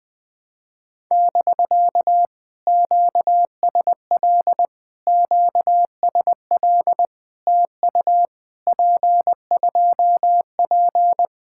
Morse code, heard as a keyed tone, about 20 wpm, 700 Hz. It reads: <BK> QSL QSL TU P2P
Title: POTA QSO Practice - 20wpm